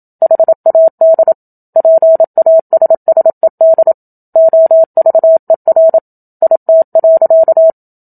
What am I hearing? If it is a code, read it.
Morse code, heard as a keyed tone, about 27 wpm, 650 Hz. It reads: HAD PASSED OVER IT.
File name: Book Two / 01-Chapter 1 - 27 WPM